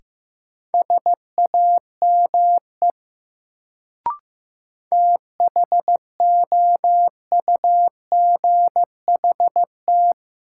Morse code, heard as a keyed tone, about 15 wpm, 700 Hz. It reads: SAME THOUGHT
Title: Sets of 2 Words - New Word Drill - 15wpm